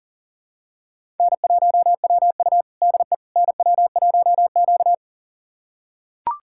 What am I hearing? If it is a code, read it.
Morse code, heard as a keyed tone, about 40 wpm, 700 Hz. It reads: N1WU DE NW1Q
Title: USA Callsign DE Callsign - 4 Characters - 40wpm